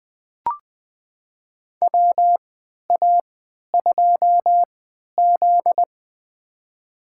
Morse code, heard as a keyed tone, about 20 wpm, spaced as for 11 wpm, 700 Hz. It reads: WA2Z
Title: USA Call Signs - 4 Characters - 3x Character Spacing - 20wpm